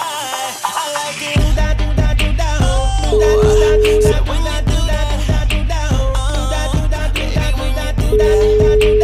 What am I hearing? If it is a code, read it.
Ringback tone (Europe, 425 Hz), with noise.